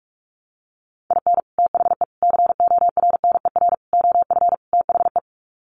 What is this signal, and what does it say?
Morse code, heard as a keyed tone, about 44 wpm, 700 Hz. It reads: IN THE CORNER OF THE
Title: Sets of 5 Words Encore 45wpm